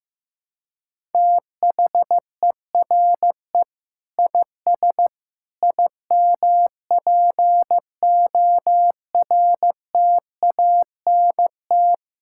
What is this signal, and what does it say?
Morse code, heard as a keyed tone, about 15 wpm, 700 Hz. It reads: THERE IS IMPORTANT
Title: Sentences from Top 1000 Words - Review Part 6 - 15wpm